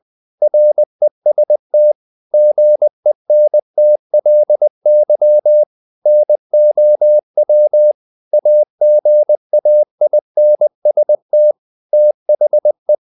Morse code, heard as a keyed tone, about 20 wpm, 600 Hz. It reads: REST GENTLY NOW AGAINST THE
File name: the old man and the sea / the old man and the sea part 1 20wpm